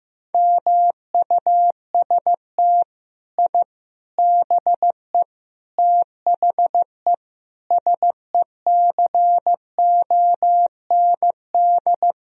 Morse code, heard as a keyed tone, about 16 wpm, 700 Hz. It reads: MUST I BE THE SECOND
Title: Sentences from Top 1000 Words - Review Part 4 - 15wpm